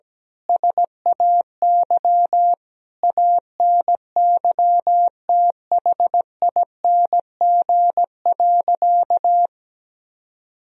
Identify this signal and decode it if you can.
Morse code, heard as a keyed tone, about 17 wpm, 700 Hz. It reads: SAY ANYTHING.
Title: Sentences from Top 1000 Words - Review Part 2 - 17wpm